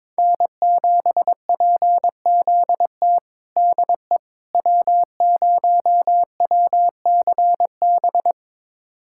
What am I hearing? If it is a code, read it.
Morse code, heard as a keyed tone, about 21 wpm, 700 Hz. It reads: N7PZT DE W0WCB
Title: USA Callsign DE Callsign 22wpm